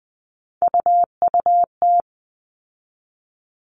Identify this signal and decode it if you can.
Morse code, heard as a keyed tone, about 20 wpm, 700 Hz. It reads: UUT
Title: All Cut Numbers - 3-Digit 20wpm